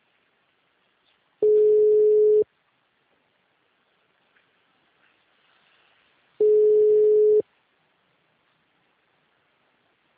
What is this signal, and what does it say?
Ringback tone (Europe, 425 Hz).
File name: out- -2046- - - .597